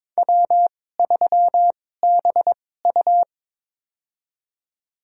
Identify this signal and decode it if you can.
Morse code, heard as a keyed tone, about 22 wpm, spaced as for 16 wpm, 700 Hz. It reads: W3BU
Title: USA Call Signs - 4 Characters - 2x Character Spacing - 22wpm